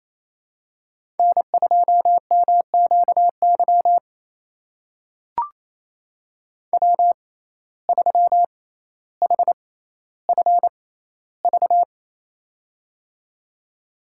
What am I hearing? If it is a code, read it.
Morse code, heard as a keyed tone, about 28 wpm, spaced as for 14 wpm, 700 Hz. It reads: N2MQY W3HFV
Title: USA Call Signs - 5 Characters - 6x Character Spacing - 28wpm